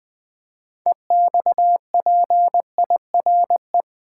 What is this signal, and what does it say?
Morse code, heard as a keyed tone, about 20 wpm, 700 Hz. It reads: EXPIRE
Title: Build A Word - 2 Letters At A Time - 6-Letter Words - 20wpm